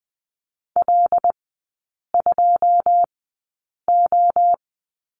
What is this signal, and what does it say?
Morse code, heard as a keyed tone, about 20 wpm, spaced as for 8 wpm, 700 Hz. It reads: L2O
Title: 121 Lesson 20e - Number 3 - Three Characters 20wpm